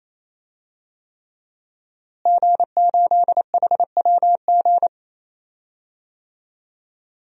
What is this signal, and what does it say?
Morse code, heard as a keyed tone, about 28 wpm, 700 Hz. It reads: G8HWG
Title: DXCC-100 Call Signs - 5 Characters - Repeated 3x - 28wpm